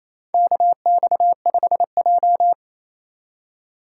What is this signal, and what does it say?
Morse code, heard as a keyed tone, about 28 wpm, 700 Hz. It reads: KX5J
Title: USA Call Signs - 3x Character Spacing - 28wpm